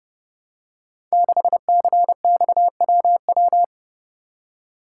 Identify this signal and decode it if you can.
Morse code, heard as a keyed tone, about 30 wpm, 700 Hz. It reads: 6CXWW
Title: Code Groups - 5 Letter:Numbers - 30wpm